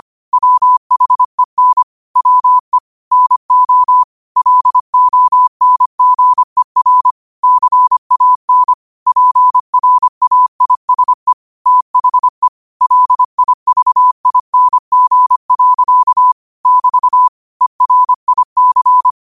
Morse code, heard as a keyed tone, about 25 wpm, 1000 Hz. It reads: WHEN WE NO LONGER CAN PRAISE THE LIVING. <BT> ERIC